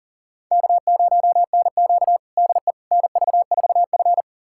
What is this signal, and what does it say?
Morse code, heard as a keyed tone, about 37 wpm, 700 Hz. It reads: K0NQ DE NV4F
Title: USA Callsign DE Callsign - 4 Characters - 40wpm